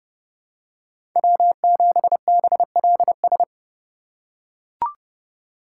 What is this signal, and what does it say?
Morse code, heard as a keyed tone, about 32 wpm, 700 Hz. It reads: W7BLS
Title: USA Call Signs - 5 Characters - 4x Character Spacing - 30wpm